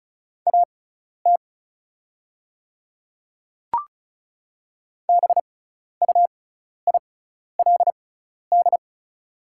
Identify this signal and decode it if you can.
Morse code, heard as a keyed tone, about 36 wpm, spaced as for 12 wpm, 690 Hz. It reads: AT BUILD
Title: 6x Character Spacing - Top 300 Words - 35wpm